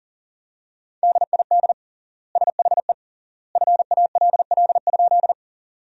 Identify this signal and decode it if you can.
Morse code, heard as a keyed tone, about 40 wpm, 700 Hz. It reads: DID SHE FALL?
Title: 3x Word-Spacing - Sentences from Top 500 Words - 40wpm